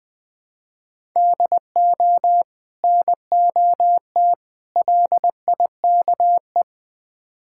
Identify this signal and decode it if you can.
Morse code, heard as a keyed tone, about 20 wpm, 700 Hz. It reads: DO NOT LIKE
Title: Build A Phrase - 3-Word Phrases - 20wpm